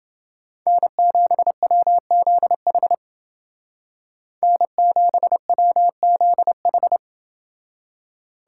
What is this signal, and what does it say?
Morse code, heard as a keyed tone, about 28 wpm, 700 Hz. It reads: N7WZH N7WZH